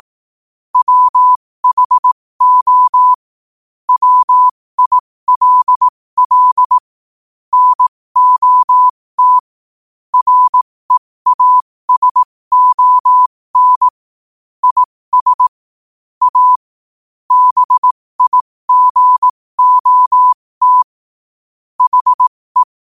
Morse code, heard as a keyed tone, about 18 wpm, spaced as for 15 wpm, 1000 Hz. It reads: WHO WILL NOT REASON IS A BIGOT HE
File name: 15wpm